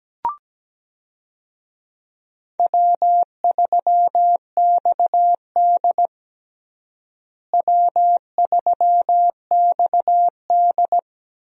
Morse code, heard as a keyed tone, about 17 wpm, 700 Hz. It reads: W3XD W3XD
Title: USA Call Signs - 4 Characters - Repeated 3x - 17wpm